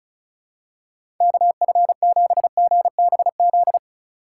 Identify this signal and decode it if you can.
Morse code, heard as a keyed tone, about 36 wpm, 700 Hz. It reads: KF7GBZ
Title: USA Call Signs - 6 Characters - 3x Character Spacing - 35wpm